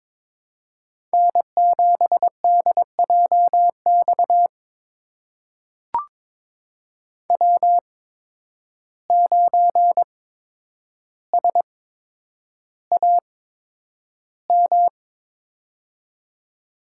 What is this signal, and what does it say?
Morse code, heard as a keyed tone, about 22 wpm, spaced as for 9 wpm, 700 Hz. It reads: N7DJX W9SAM
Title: USA Call Signs - 5 Characters - 8x Character Spacing - 22wpm